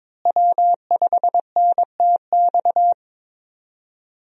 Morse code, heard as a keyed tone, about 22 wpm, 700 Hz. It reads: W5NTX
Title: USA Call Signs - 5 Characters - Repeated 2x - 22wpm